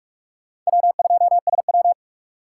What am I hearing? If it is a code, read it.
Morse code, heard as a keyed tone, about 48 wpm, 690 Hz. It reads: W2SW
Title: USA Call Signs - 4 Characters - 3x Character Spacing - 45wpm